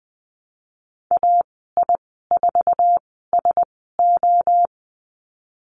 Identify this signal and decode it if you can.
Morse code, heard as a keyed tone, about 20 wpm, spaced as for 15 wpm, 700 Hz. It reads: AI4SO
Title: 048 Lesson 9g - Number 4 - Call Signs 20wpm